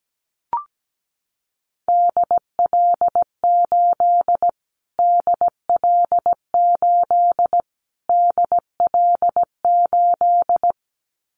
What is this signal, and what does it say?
Morse code, heard as a keyed tone, about 17 wpm, 700 Hz. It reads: DL8 DL8 DL8
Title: Callsign Prefixes for Top 25 DXCC - Repeated 3x - 17wpm